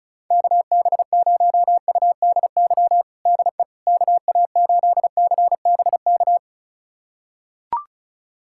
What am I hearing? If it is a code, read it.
Morse code, heard as a keyed tone, about 34 wpm, 700 Hz. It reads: KB0UDY DE KA8CBK
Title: USA Callsign DE Callsign - 6 Characters - 35wpm